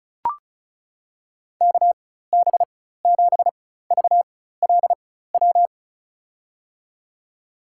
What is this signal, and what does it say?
Morse code, heard as a keyed tone, about 34 wpm, spaced as for 16 wpm, 690 Hz. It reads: KB7VLW
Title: USA Call Signs - 6 Characters - 4x Character Spacing - 35wpm